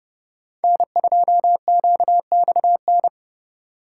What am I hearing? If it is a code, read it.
Morse code, heard as a keyed tone, about 31 wpm, 700 Hz. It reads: N2QXN
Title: USA Call Signs - 5 Characters - 3x Character Spacing - 30wpm